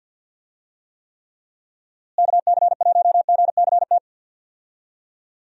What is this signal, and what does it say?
Morse code, heard as a keyed tone, about 52 wpm, 700 Hz. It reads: KC1GCT
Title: USA Call Signs - 6 Characters - Repeated 3x - 50wpm